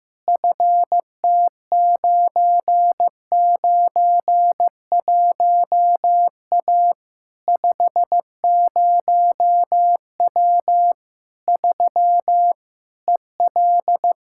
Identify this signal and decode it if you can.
Morse code, heard as a keyed tone, about 15 wpm, 700 Hz. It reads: FT991A 50W 3 EL
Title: Practice QSO - Exchange 2 - Info - 15wpm